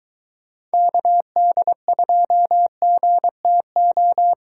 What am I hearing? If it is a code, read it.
Morse code, heard as a keyed tone, about 22 wpm, 700 Hz. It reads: KD2GTO